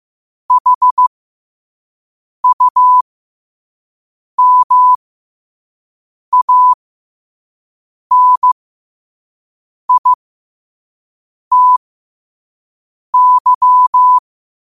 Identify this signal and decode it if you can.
Morse code, heard as a keyed tone, about 15 wpm, spaced as for 5 wpm, 1000 Hz. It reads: HUMANITY
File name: qotdfiles / 5wpm